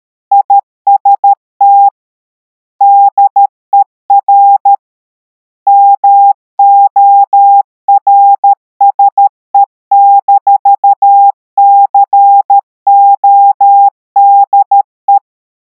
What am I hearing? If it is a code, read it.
Morse code, heard as a keyed tone, about 13 wpm, 800 Hz. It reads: IST DER MORSE-CODE
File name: MorseIt 800HZ 13WPM